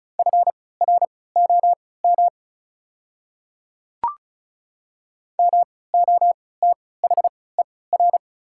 Morse code, heard as a keyed tone, about 34 wpm, spaced as for 20 wpm, 690 Hz. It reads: FROM MOTHER
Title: 3x Character Spacing - Top 500 Words - 35wpm